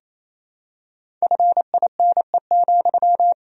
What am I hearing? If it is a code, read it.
Morse code, heard as a keyed tone, about 28 wpm, 700 Hz. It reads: FINE,
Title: Sentences from Top 500 Words and Abbreviations 28wpm